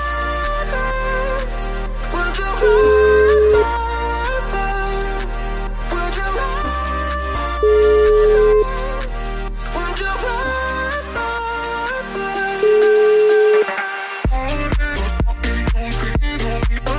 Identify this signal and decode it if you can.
Ringback tone (Europe, 425 Hz), with noise.